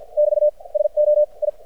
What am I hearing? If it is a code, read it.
Morse code, heard as a keyed tone, about 56 wpm, spaced as for 45 wpm, 600 Hz. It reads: <BT> SO I
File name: archived / CRSnet